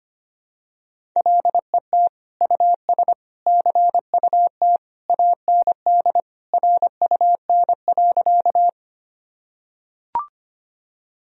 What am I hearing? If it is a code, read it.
Morse code, heard as a keyed tone, about 25 wpm, 700 Hz. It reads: LET US CUT AND RUN.
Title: Sentences from Top 1000 Words - Review Part 1 - 25wpm